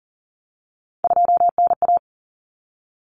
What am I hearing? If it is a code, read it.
Morse code, heard as a keyed tone, about 40 wpm, 700 Hz. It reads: 2NA
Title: Common Cut Numbers - 3-Digit 40wpm